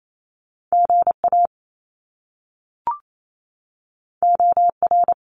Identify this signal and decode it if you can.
Morse code, heard as a keyed tone, about 28 wpm, 700 Hz. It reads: GA OR
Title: Top Two Letter Combinations in Words 28wpm